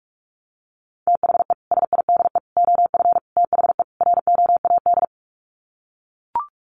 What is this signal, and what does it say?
Morse code, heard as a keyed tone, about 44 wpm, 700 Hz. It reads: THE SIDE OF THE ROAD
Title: Sets of 5 Words 45wpm